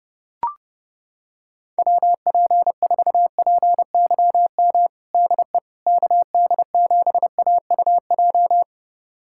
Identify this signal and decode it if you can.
Morse code, heard as a keyed tone, about 30 wpm, 700 Hz. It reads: WP4PYM DE KD7AUJ
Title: USA Callsign DE Callsign - 6 Characters - 30wpm